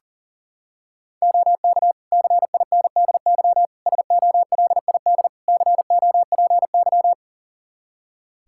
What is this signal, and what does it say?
Morse code, heard as a keyed tone, about 40 wpm, 700 Hz. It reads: OK CINDY SOLID COPY
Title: K1USN SST QSO Practice 40wpm